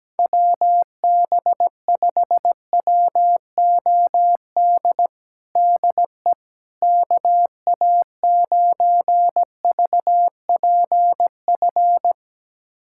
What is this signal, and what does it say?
Morse code, heard as a keyed tone, about 17 wpm, 700 Hz. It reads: WB5WOD DE KA9VPF
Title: USA Callsign DE Callsign - 6 Characters - 17wpm